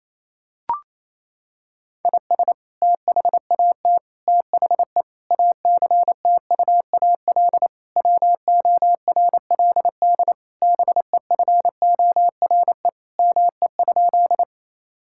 Morse code, heard as a keyed tone, about 28 wpm, 700 Hz. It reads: IS THAT THE ACTUAL WORLD BEFORE ME?
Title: Sentences from Top 1000 Words - Review Part 5 - 28wpm